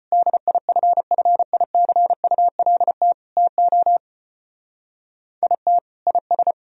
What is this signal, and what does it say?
Morse code, heard as a keyed tone, about 31 wpm, 700 Hz. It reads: DIFFICULT TO IT IS
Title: Speed-Racing - Sets of 4 Words 25wpm